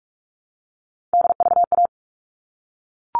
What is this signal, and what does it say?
Morse code, heard as a keyed tone, about 44 wpm, 700 Hz. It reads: DVA
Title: All Cut Numbers - 3-Digit 45wpm